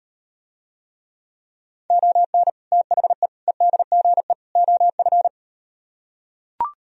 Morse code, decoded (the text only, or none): ON THE EDGE OF